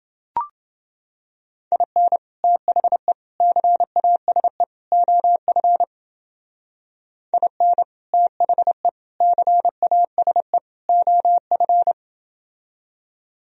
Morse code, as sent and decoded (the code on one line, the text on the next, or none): .. -. / - .... / -.-. .- ... / --- ..-. / .. -. / - .... / -.-. .- ... / --- ..-.
IN THE CASE OF IN THE CASE OF